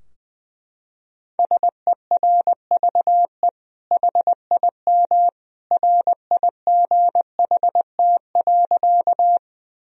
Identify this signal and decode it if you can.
Morse code, heard as a keyed tone, about 20 wpm, 700 Hz. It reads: SERVE HIM RIGHT.
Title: Sentences from Top 1000 Words - Review Part 2 - 20wpm